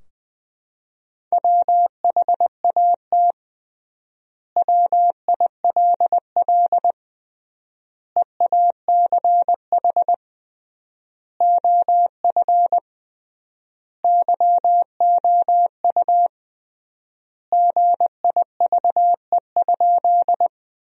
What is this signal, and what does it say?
Morse code, heard as a keyed tone, about 20 wpm, 700 Hz. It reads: WHAT WILL EACH OF YOU GIVE?
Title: 3x Word-Spacing - Sentences from Top 200 Words - 20wpm